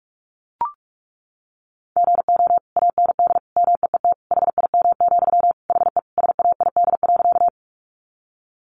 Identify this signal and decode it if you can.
Morse code, heard as a keyed tone, about 44 wpm, 700 Hz. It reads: GO AND MEET HIM, HE SAID.
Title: Sentences from Top 1100 Words 45wpm